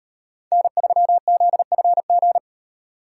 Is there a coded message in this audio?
Yes: Morse code from 0.5 to 2.4 s, heard as a keyed tone, about 35 wpm, 700 Hz. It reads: N3ZFG